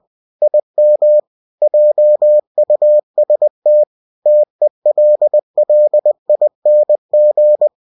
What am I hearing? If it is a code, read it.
Morse code, heard as a keyed tone, about 20 wpm, 600 Hz. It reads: IM JUST TELLING
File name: cw-books / winnie-the-pooh / winnie-20wpm-page-0080